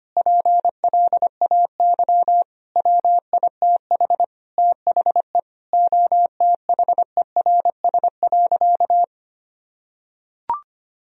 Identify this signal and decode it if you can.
Morse code, heard as a keyed tone, about 25 wpm, 700 Hz. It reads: PLAY WITH THE OTHERS.
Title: Sentences from Top 1000 Words - Review Part 2 - 25wpm